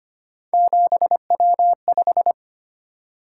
Morse code, heard as a keyed tone, about 29 wpm, 700 Hz. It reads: 7W5
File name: Speed-Racing - Three Letter-Number Combinations - 25wpm